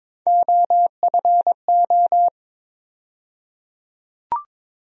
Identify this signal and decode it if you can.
Morse code, heard as a keyed tone, about 23 wpm, 700 Hz. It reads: OFO
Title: USA Call Sign Postfixes - 3 Letters 22wpm